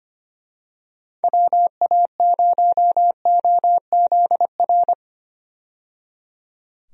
Morse code, heard as a keyed tone, about 25 wpm, 700 Hz. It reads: WA0OZR
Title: USA Call Signs - 6 Characters - 25wpm